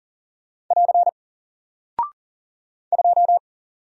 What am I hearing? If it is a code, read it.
Morse code, heard as a keyed tone, about 38 wpm, 700 Hz. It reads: <AR> 2
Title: Single Character (Letter:Number:Punctuation:Prosign) Alt 40wpm